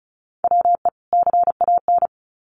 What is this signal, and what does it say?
Morse code, heard as a keyed tone, about 34 wpm, 700 Hz. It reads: WE CAN
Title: Sets of 2 Words 35wpm